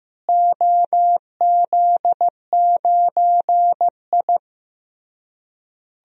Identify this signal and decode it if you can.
Morse code, heard as a keyed tone, about 15 wpm, 700 Hz. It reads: OZ9I
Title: DXCC-100 Call Signs - 4 Characters - Repeated 3x - 15wpm